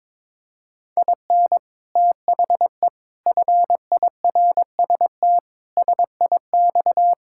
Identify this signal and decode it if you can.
Morse code, heard as a keyed tone, about 22 wpm, 700 Hz. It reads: IN THE FIRST SIX
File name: practice / 22wpm / Sentences with 8-Word Phrases - 22wpm